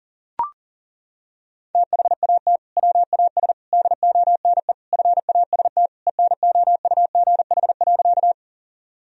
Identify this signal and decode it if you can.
Morse code, heard as a keyed tone, about 37 wpm, 700 Hz. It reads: THAT WAS DONE FAST ENOUGH.
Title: Sentences from Top 1000 Words - Review Part 2 - 40wpm